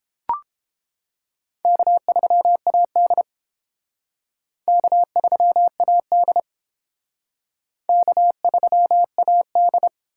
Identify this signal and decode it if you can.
Morse code, heard as a keyed tone, about 30 wpm, 700 Hz. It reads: K3AD K3AD K3AD